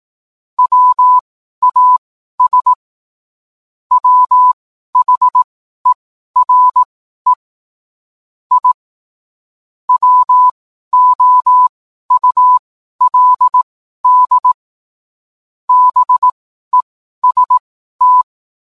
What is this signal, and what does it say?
Morse code, heard as a keyed tone, about 18 wpm, spaced as for 12 wpm, 1000 Hz. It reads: WAS WHERE I WOULD BEST